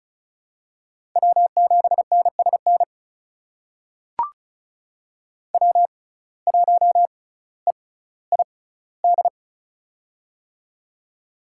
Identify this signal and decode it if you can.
Morse code, heard as a keyed tone, about 34 wpm, spaced as for 18 wpm, 700 Hz. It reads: W7NSN W1EID
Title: USA Call Signs - 5 Characters - 6x Character Spacing - 35wpm